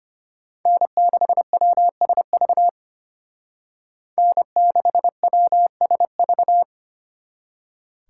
Morse code, heard as a keyed tone, about 33 wpm, 700 Hz. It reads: N6WSV N6WSV